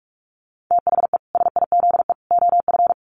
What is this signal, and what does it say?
Morse code, heard as a keyed tone, about 44 wpm, 700 Hz. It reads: THE SIZE OF
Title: Sets of 3 Words 45wpm